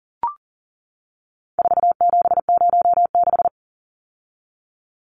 Morse code, heard as a keyed tone, about 40 wpm, 700 Hz. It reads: 4706
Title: Numbers - 4-Digit 40wpm